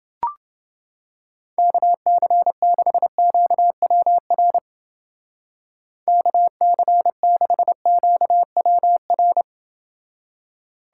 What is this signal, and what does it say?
Morse code, heard as a keyed tone, about 29 wpm, 700 Hz. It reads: KC6QWR KC6QWR